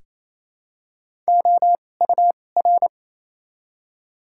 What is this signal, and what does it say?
Morse code, heard as a keyed tone, about 28 wpm, 700 Hz. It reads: O U R
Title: 2x Character Spacing - Top 200 Words - 28wpm